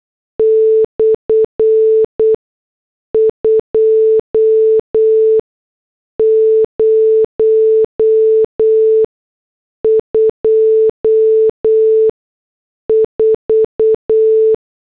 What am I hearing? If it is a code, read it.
Morse code, heard as a keyed tone, about 8 wpm, spaced as for 6 wpm, 440 Hz. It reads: /2024